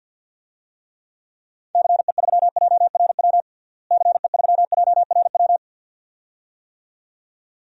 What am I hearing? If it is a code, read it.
Morse code, heard as a keyed tone, about 52 wpm, 700 Hz. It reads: CE3JRW CE3JRW
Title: DXCC-100 Call Signs - 6 Characters - Repeated 2x - 50wpm